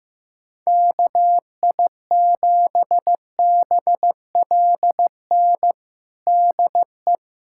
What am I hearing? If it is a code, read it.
Morse code, heard as a keyed tone, about 15 wpm, 700 Hz. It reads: KI7BLN DE
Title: USA Callsign DE Callsign 15wpm